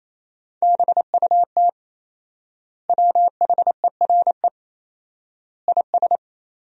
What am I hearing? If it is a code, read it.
Morse code, heard as a keyed tone, about 28 wpm, 700 Hz. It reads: BUT WHERE IS
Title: 4x Word-Spacing - Sentences from Top 200 Words - 28wpm